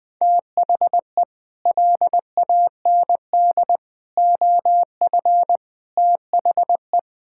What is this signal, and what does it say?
Morse code, heard as a keyed tone, about 20 wpm, 700 Hz. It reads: THE LAND OF THE
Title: Sets of 8 Words - 20wpm